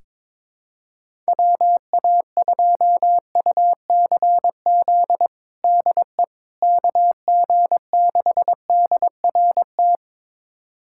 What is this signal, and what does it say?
Morse code, heard as a keyed tone, about 22 wpm, 700 Hz. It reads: WA2UCZ DE KG6DRT
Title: USA Callsign DE Callsign - 6 Characters - 22wpm